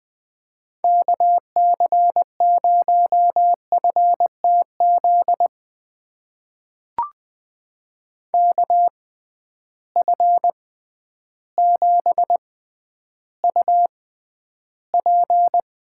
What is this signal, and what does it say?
Morse code, heard as a keyed tone, about 20 wpm, spaced as for 11 wpm, 700 Hz. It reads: KC0FTZ KF7UP
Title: USA Call Signs - 6x Character Spacing - 20wpm